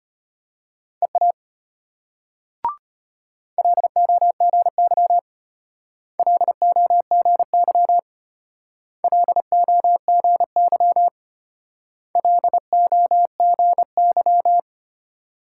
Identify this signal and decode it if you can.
Morse code, heard as a keyed tone, about 32 wpm, 700 Hz. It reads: EA LOGY LOGY LOGY LOGY